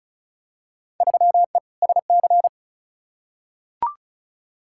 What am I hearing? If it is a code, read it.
Morse code, heard as a keyed tone, about 34 wpm, 700 Hz. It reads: 3E SC
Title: Field Day Exchanges Encore - 35wpm